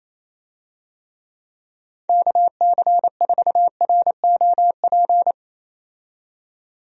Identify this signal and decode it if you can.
Morse code, heard as a keyed tone, about 28 wpm, 700 Hz. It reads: KC4ROP
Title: USA Call Signs - 6 Characters - 28wpm